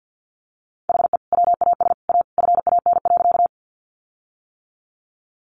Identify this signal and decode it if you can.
Morse code, heard as a keyed tone, about 50 wpm, 700 Hz. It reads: HE WAS A FAN.
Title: Sentences from Top 1300 Words 50wpm